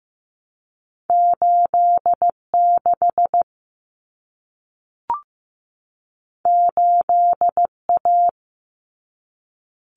Morse code, heard as a keyed tone, about 15 wpm, 700 Hz. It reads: 86 8A
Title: Common Cut Numbers - 2-Digit 15wpm